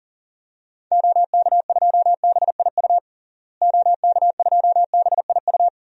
Morse code, heard as a keyed tone, about 40 wpm, 700 Hz. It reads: OK2BIU OK2BIU
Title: DXCC-100 Call Signs - 6 Characters - Repeated 2x - 40wpm